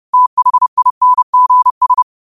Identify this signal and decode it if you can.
Morse code, heard as a keyed tone, about 30 wpm, 1000 Hz. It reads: THINGS
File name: qotdfiles / 30wpm